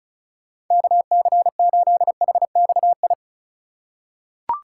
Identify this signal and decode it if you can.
Morse code, heard as a keyed tone, about 36 wpm, 700 Hz. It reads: KC8HXI
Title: USA Call Signs - 4x Character Spacing - 35wpm